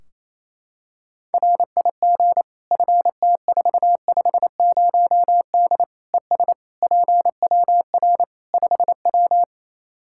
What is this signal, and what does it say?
Morse code, heard as a keyed tone, about 28 wpm, 700 Hz. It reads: RIG FT450D ES PWR 5W
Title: Practice QSO - Exchange 2 - Info - 28wpm